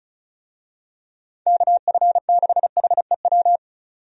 Morse code, heard as a keyed tone, about 36 wpm, 700 Hz. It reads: KF6HEW
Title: USA Call Signs - 6 Characters - 35wpm